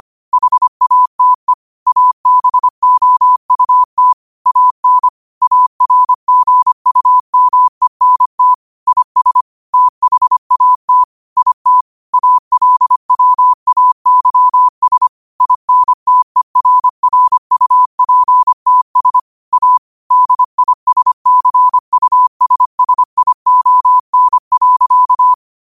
Morse code, heard as a keyed tone, about 25 wpm, 1000 Hz. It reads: HATE ABOUT AN ARGUMENT IS THAT IT ALWAYS INTERRUPTS A DISCUSSION.